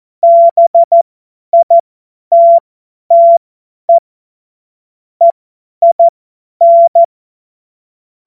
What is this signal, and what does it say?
Morse code, heard as a keyed tone, about 14 wpm, spaced as for 10 wpm, 680 Hz. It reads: BITTE EIN
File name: Morsekurs / Witze / Witz 16. Februar 2021, 50 F70